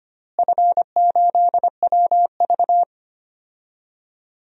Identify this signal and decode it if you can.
Morse code, heard as a keyed tone, about 25 wpm, 700 Hz. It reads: F8WV
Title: DXCC-100 Call Signs - 4 Characters - Repeated 3x - 25wpm